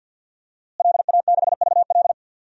Morse code, heard as a keyed tone, about 48 wpm, 700 Hz. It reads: LA6VL
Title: DXCC-100 Call Signs - 5 Characters - Repeated 3x - 50wpm